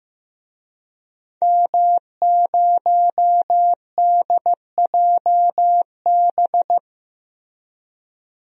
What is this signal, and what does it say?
Morse code, heard as a keyed tone, about 15 wpm, 700 Hz. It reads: M0DJB
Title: DXCC-100 Call Signs - 5 Characters - Repeated 3x - 15wpm